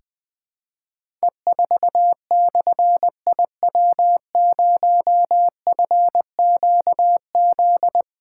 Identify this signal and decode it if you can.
Morse code, heard as a keyed tone, about 20 wpm, 700 Hz. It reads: E4/IW0FQZ
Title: Difficult Call Signs 20wpm